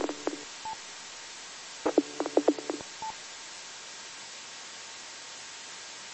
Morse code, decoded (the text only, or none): none